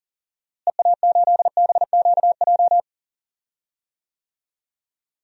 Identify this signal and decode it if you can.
Morse code, heard as a keyed tone, about 40 wpm, 700 Hz. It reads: EA8BQJ
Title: DXCC-100 Call Signs - 6 Characters - Repeated 2x - 40wpm